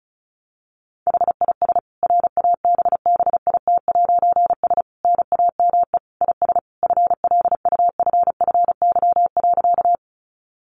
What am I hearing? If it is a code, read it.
Morse code, heard as a keyed tone, about 34 wpm, 700 Hz. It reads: HIS RABBIT'S NAME IS FLUFFY.
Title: Easy Phrases 35wpm